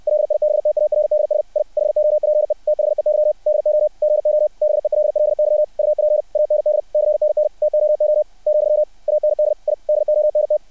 Morse code, heard as a keyed tone, about 60 wpm, 600 Hz. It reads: LIBTARDS I HOPE THEY DO GO BELLY UP AND CNN TOO , AND I DONT
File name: archived / CRSnet